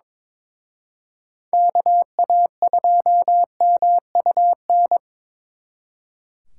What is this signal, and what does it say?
Morse code, heard as a keyed tone, about 25 wpm, 700 Hz. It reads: KA2MUN
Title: Speed Racing - USA 2x3 Call Signs 22wpm